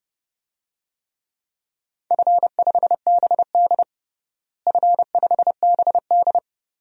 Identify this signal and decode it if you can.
Morse code, heard as a keyed tone, about 29 wpm, 700 Hz. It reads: F5BD F5BD
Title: DXCC-100 Call Signs - 4 Characters - Repeated 3x - 30wpm